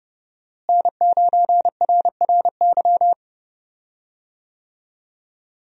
Morse code, heard as a keyed tone, about 30 wpm, 700 Hz. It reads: N9RRY
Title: USA Call Signs - 5 Characters - Repeated 2x - 30wpm